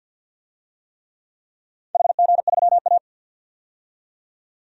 Morse code, heard as a keyed tone, about 49 wpm, 700 Hz. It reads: HG3A
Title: DXCC-100 Call Signs - 4 Characters - 50wpm